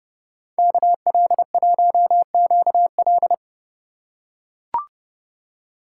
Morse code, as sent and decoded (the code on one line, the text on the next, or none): -.- .-.. .---- --.- .-..
KL1QL